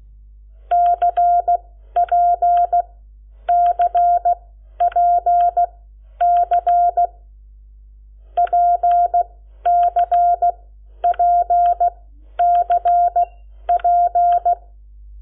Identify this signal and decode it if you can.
Morse code, heard as a keyed tone, about 16 wpm, spaced as for 11 wpm, 700 Hz. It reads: CPCPC PCPCP
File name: Lecciones / CLASE Nº7